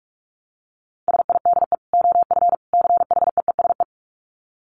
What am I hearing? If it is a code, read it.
Morse code, heard as a keyed tone, about 44 wpm, 700 Hz. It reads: SIDE OF CHEESE
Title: Dit Challenge 45wpm